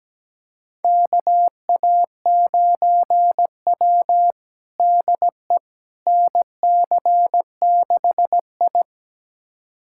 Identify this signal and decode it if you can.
Morse code, heard as a keyed tone, about 17 wpm, 700 Hz. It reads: KA9W DE NC6I
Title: USA Callsign DE Callsign - 4 Characters - 17wpm